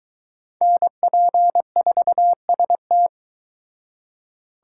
Morse code, heard as a keyed tone, about 22 wpm, 700 Hz. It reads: NP4ST